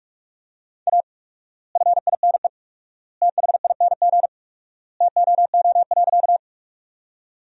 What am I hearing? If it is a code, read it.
Morse code, heard as a keyed tone, about 43 wpm, 700 Hz. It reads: A FINE THING TOO.
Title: 4x Word-Spacing - Sentences from Top 500 Words - 45wpm